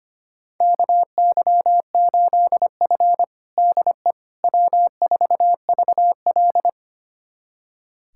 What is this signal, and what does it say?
Morse code, heard as a keyed tone, about 25 wpm, 700 Hz. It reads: KY8F DE W4VL
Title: USA Callsign DE Callsign - 4 Characters - 25wpm